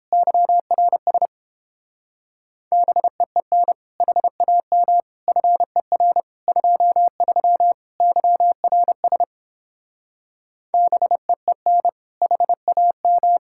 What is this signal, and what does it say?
Morse code, heard as a keyed tone, about 31 wpm, 700 Hz. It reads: YRS BEEN HAM FER 23 YRS BEEN HAM